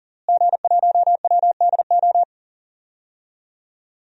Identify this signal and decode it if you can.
Morse code, heard as a keyed tone, about 38 wpm, 700 Hz. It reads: G1WDO
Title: DXCC-100 Call Signs - 5 Characters - Repeated 2x - 40wpm